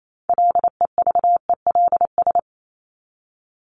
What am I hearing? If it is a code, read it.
Morse code, heard as a keyed tone, about 28 wpm, 700 Hz. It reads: LEVELS
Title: ES:S Suffix - Top 100 Words with Base Words 28wpm